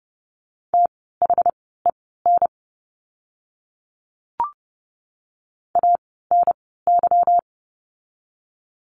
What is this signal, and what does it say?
Morse code, heard as a keyed tone, about 30 wpm, spaced as for 17 wpm, 700 Hz. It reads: THEN ANY
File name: course / 30:10wpm / 230 Lesson 37f - Letter x - Words 30wpm